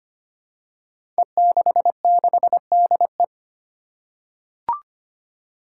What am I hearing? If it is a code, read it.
Morse code, heard as a keyed tone, about 25 wpm, 700 Hz. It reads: E66DE
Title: Five Challenging Letter-Number Combinations 25wpm